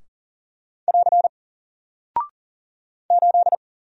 Morse code, heard as a keyed tone, about 40 wpm, 700 Hz. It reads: <AR> 8
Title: Single Character (Letter:Number:Punctuation:Prosign) Alt 40wpm